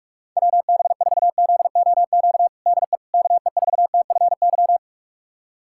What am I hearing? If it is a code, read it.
Morse code, heard as a keyed tone, about 42 wpm, 700 Hz. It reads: WB4ZOQ DE KE4TFY
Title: USA Callsign DE Callsign 45wpm